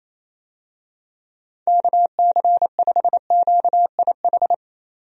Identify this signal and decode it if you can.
Morse code, heard as a keyed tone, about 28 wpm, 700 Hz. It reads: KC5QIH
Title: USA Call Signs - 6 Characters - 28wpm